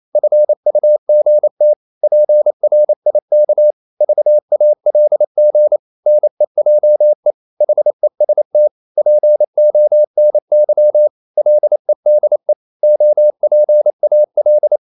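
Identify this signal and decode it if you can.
Morse code, heard as a keyed tone, about 28 wpm, 600 Hz. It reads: FUGT PRIK VALG NEJE HEST PONY LEDE OPAL
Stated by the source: Danish short words sent in Morse at 28 wpm, from CW med Gnister.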